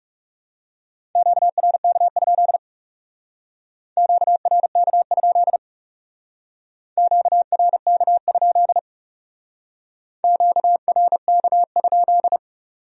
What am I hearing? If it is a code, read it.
Morse code, heard as a keyed tone, about 36 wpm, 700 Hz. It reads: QRK? QRK? QRK? QRK?
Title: Speed-Racing Q-Codes - 30wpm